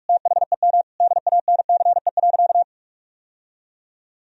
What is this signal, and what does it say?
Morse code, heard as a keyed tone, about 46 wpm, 700 Hz. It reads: THEM DANCE.
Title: Sentences from Top 1000 Words - Review Part 4 - 45wpm